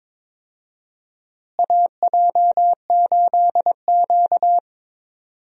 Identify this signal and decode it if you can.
Morse code, heard as a keyed tone, about 22 wpm, 700 Hz. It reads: AJ8Q
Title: USA Call Signs - 4 Characters - Repeated 3x - 22wpm